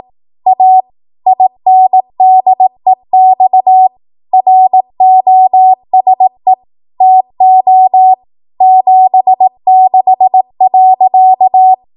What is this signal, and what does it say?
Morse code, heard as a keyed tone, about 18 wpm, 750 Hz. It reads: A INDEX ROSE TO 76.
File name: Archive / 18 WPM / 18WPM